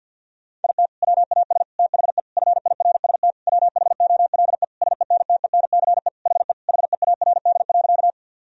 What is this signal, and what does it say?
Morse code, heard as a keyed tone, about 53 wpm, 700 Hz. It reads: IT WAS THE FIRST WHOLE SENTENCE HE HEARD.
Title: Sentences from Top 1000 Words - Review Part 1 - 50wpm